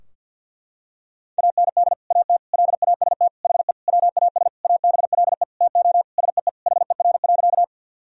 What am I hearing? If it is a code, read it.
Morse code, heard as a keyed tone, about 52 wpm, 700 Hz. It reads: AND AT LAST HE WAS ABLE TO SEE HER.
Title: Sentences from Top 1000 Words - Review Part 2 - 50wpm